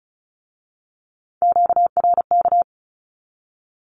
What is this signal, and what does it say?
Morse code, heard as a keyed tone, about 36 wpm, 700 Hz. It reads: QRK
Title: Q-Codes with Spoken Meaning - 35wpm